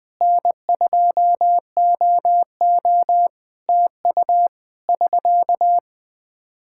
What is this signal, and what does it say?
Morse code, heard as a keyed tone, about 20 wpm, 700 Hz. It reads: N2OO TU <SK>
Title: K1USN SST QSO Practice 20wpm